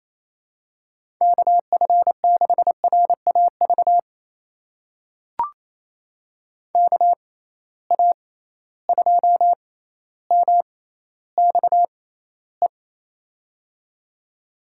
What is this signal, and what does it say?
Morse code, heard as a keyed tone, about 28 wpm, spaced as for 14 wpm, 700 Hz. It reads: KF6RAV KA2MXE
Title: USA Call Signs - 6x Character Spacing - 28wpm